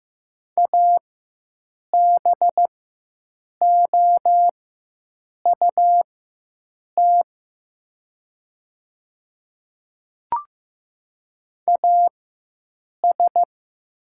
Morse code, heard as a keyed tone, about 15 wpm, spaced as for 7 wpm, 700 Hz. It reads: ABOUT AS
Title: 4x Character Spacing - Top 200 Words - 15wpm